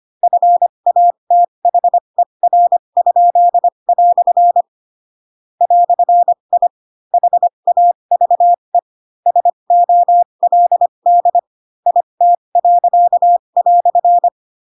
Morse code, heard as a keyed tone, about 25 wpm, spaced as for 22 wpm, 700 Hz. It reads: FATHER?" "I HAVE SOLD IT."